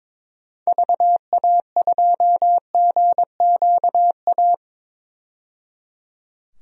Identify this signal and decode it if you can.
Morse code, heard as a keyed tone, about 22 wpm, 700 Hz. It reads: VA2GQA
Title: DXCC-100 Call Signs - 6 Characters - 22wpm